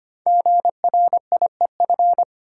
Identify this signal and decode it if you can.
Morse code, heard as a keyed tone, about 25 wpm, 700 Hz. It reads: GRIEF
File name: 5 Letter Words - First 3 Letters Given - 25wpm